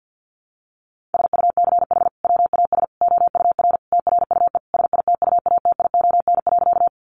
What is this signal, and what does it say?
Morse code, heard as a keyed tone, about 50 wpm, 700 Hz. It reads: SUCH WAS OUR TRUE SITUATION.
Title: Sentences from Top 1200 Words 50wpm